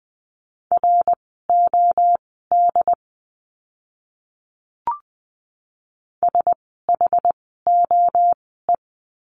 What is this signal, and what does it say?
Morse code, heard as a keyed tone, about 20 wpm, spaced as for 14 wpm, 700 Hz. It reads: ROD SHOE
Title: 068 Lesson 12f - Letter d - Words 20wpm